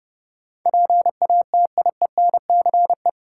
Morse code, heard as a keyed tone, about 30 wpm, 700 Hz. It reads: PATIENCE
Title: Sentences from Top 1300 Words - New Word Drill - 30wpm